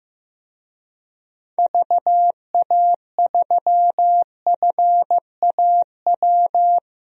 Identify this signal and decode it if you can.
Morse code, heard as a keyed tone, about 16 wpm, 700 Hz. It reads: VA3FAW
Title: DXCC-100 Call Signs - 6 Characters - Repeated 3x - 15wpm